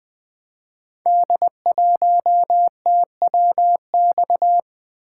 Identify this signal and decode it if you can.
Morse code, heard as a keyed tone, about 20 wpm, 700 Hz. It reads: D1TWX
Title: Code Groups - 5 Letter:Numbers - 20wpm